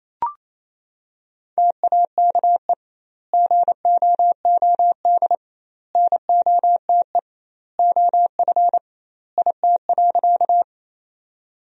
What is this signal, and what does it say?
Morse code, heard as a keyed tone, about 28 wpm, 700 Hz. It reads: TAKE GOOD NOTE OF IT.
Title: 2x Word-Spacing - Sentences from Top 500 Words - 28wpm